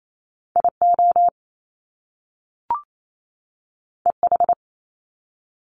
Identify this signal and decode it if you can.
Morse code, heard as a keyed tone, about 28 wpm, 700 Hz. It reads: IO EH
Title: Top Two Letter Combinations in Words 28wpm